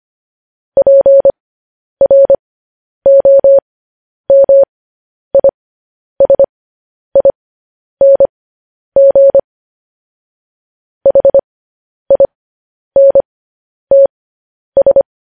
Morse code, heard as a keyed tone, about 25 wpm, spaced as for 10 wpm, 550 Hz. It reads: PROMISING HINTS
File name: podcast / News headlines-1x-25x10